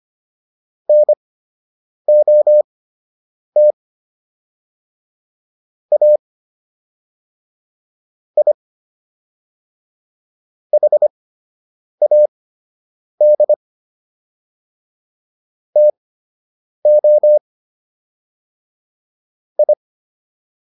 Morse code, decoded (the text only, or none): NOT A I HAD TO I